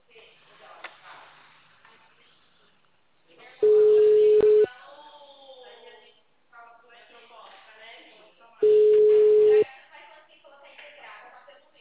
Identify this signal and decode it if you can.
Ringback tone (Europe, 425 Hz), with speech.